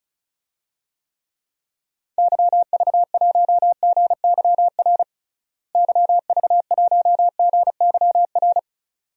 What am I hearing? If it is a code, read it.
Morse code, heard as a keyed tone, about 36 wpm, 700 Hz. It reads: YV1GYR YV1GYR
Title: DXCC-100 Call Signs - 6 Characters - Repeated 2x - 35wpm